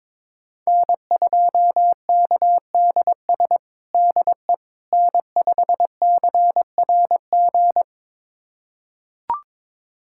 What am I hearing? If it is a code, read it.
Morse code, heard as a keyed tone, about 21 wpm, 700 Hz. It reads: N2KDS DE N5CRG
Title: USA Callsign DE Callsign - 5 Characters - 22wpm